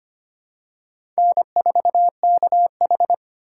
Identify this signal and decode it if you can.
Morse code, heard as a keyed tone, about 25 wpm, 700 Hz. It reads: N4KH
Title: K1USN SST QSO Practice - 25wpm